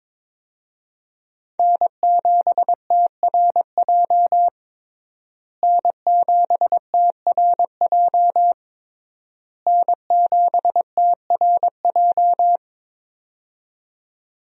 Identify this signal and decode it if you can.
Morse code, heard as a keyed tone, about 21 wpm, 700 Hz. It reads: N7TRJ N7TRJ N7TRJ
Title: USA Call Signs - 5 Characters - Repeated 3x - 22wpm